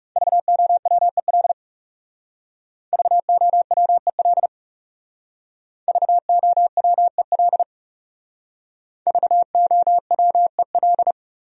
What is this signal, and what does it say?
Morse code, heard as a keyed tone, about 39 wpm, 700 Hz. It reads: VOWEL VOWEL VOWEL VOWEL